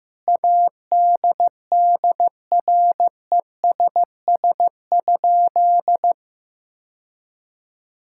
Morse code, heard as a keyed tone, about 15 wpm, 700 Hz. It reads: ADDRESS?
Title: Sentences from Top 1000 Words - Review Part 6 - 15wpm